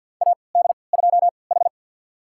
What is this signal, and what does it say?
Morse code, heard as a keyed tone, about 47 wpm, spaced as for 28 wpm, 700 Hz. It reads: AD2H
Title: USA Call Signs - 4 Characters - 3x Character Spacing - 50wpm